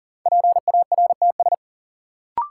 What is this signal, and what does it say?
Morse code, heard as a keyed tone, about 40 wpm, 700 Hz. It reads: PARTS
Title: Sentences from Top 1000 Words - Review Part 6 - New Word Drill - 40wpm